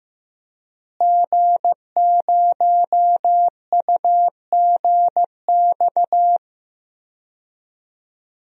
Morse code, heard as a keyed tone, about 15 wpm, 700 Hz. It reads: G0UGX
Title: DXCC-100 Call Signs - 5 Characters - Repeated 2x - 15wpm